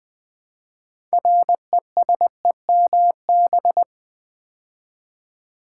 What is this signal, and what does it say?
Morse code, heard as a keyed tone, about 20 wpm, 700 Hz. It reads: RESEMB
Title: Build A Word - 3 Letters At A Time - 8-Letter Words - 20wpm